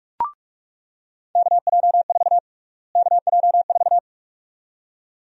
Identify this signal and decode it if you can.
Morse code, heard as a keyed tone, about 42 wpm, 700 Hz. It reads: KJ4 KJ4
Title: USA Call Sign Prefixes - Repeated 2x - 45wpm